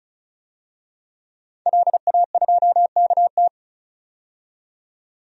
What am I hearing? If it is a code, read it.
Morse code, heard as a keyed tone, about 36 wpm, 700 Hz. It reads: LA2KT
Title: DXCC-100 Call Signs - 5 Characters - 35wpm